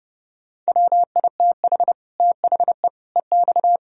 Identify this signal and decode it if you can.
Morse code, heard as a keyed tone, about 40 wpm, 700 Hz. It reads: WITH THE EX